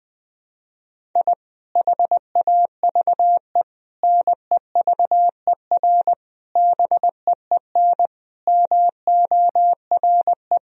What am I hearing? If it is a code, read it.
Morse code, heard as a keyed tone, about 20 wpm, 700 Hz. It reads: I HAVE NEVER BEEN MORE
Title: Sentences from Top 1000 Words - Review Part 1 - 20wpm